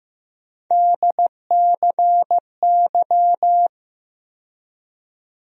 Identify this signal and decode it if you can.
Morse code, heard as a keyed tone, about 15 wpm, 700 Hz. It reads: DCY
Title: USA Call Sign Postfixes - 3 Letters 15wpm